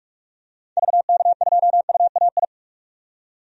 Morse code, heard as a keyed tone, about 46 wpm, 700 Hz. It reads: VK2UAI
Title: DXCC-100 Call Signs - 6 Characters - 45wpm